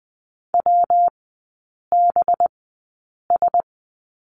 Morse code, heard as a keyed tone, about 20 wpm, spaced as for 8 wpm, 700 Hz. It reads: WBS
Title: 184 Lesson 30e - Letter b - Three Characters 20wpm